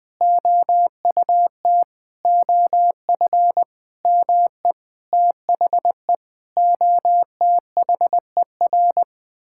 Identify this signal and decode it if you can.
Morse code, heard as a keyed tone, about 20 wpm, 700 Hz. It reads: OUT OF ME THE OTHER
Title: Sentences from Top 1000 Words - Review Part 5 - 20wpm